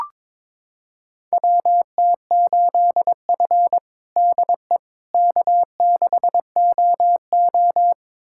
Morse code, heard as a keyed tone, about 22 wpm, 700 Hz. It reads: WT8F DE K6OO
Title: USA Callsign DE Callsign - 4 Characters - 22wpm